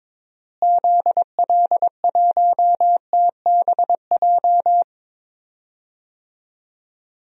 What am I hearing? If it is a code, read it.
Morse code, heard as a keyed tone, about 22 wpm, 700 Hz. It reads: ZL1TBJ
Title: DXCC-100 Call Signs - 6 Characters - Repeated 2x - 22wpm